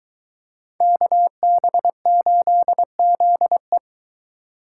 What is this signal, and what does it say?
Morse code, heard as a keyed tone, about 22 wpm, 700 Hz. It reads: KB8ZE